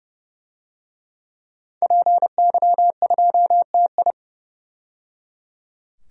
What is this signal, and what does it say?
Morse code, heard as a keyed tone, about 30 wpm, 700 Hz. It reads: PY2TI
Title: DXCC-100 Call Signs - 5 Characters - 30wpm